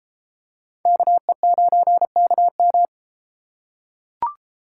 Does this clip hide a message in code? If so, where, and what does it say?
Yes: Morse code from 0.9 to 2.9 s, heard as a keyed tone, about 33 wpm, 700 Hz. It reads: KE9KM